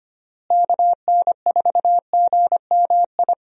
Morse code, heard as a keyed tone, about 25 wpm, 700 Hz. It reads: KN4GMI
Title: USA Call Signs - 6 Characters - Repeated 3x - 25wpm